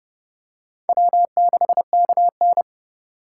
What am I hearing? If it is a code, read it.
Morse code, heard as a keyed tone, about 32 wpm, 700 Hz. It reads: W6KN
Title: USA Call Signs - 4 Characters - 4x Character Spacing - 30wpm